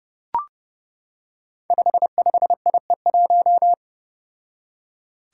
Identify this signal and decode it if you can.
Morse code, heard as a keyed tone, about 30 wpm, 700 Hz. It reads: 55IE1
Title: Five Challenging Letter-Number Combinations 30wpm